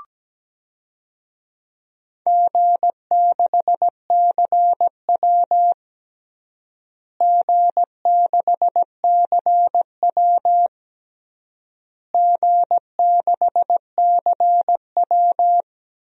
Morse code, heard as a keyed tone, about 17 wpm, 700 Hz. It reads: G6CW G6CW G6CW
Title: DXCC-100 Call Signs - 4 Characters - Repeated 3x - 17wpm